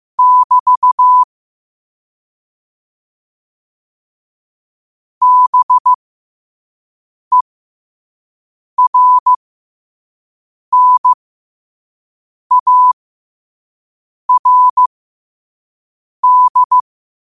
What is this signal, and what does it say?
Morse code, heard as a keyed tone, about 15 wpm, spaced as for 5 wpm, 1000 Hz. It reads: <BT> BERNARD
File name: qotdfiles / 5wpm